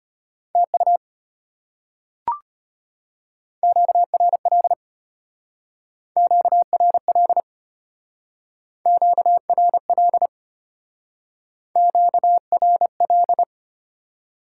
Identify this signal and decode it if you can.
Morse code, heard as a keyed tone, about 31 wpm, 700 Hz. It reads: TU QRL QRL QRL QRL
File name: practice / 25wpm / Speed-Racing Top 20 QSO Elements - 25wpm